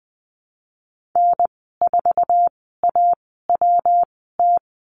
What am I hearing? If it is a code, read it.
Morse code, heard as a keyed tone, about 20 wpm, spaced as for 15 wpm, 700 Hz. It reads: N4AWT
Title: 122 Lesson 20g - Number 3 - Call Signs 20wpm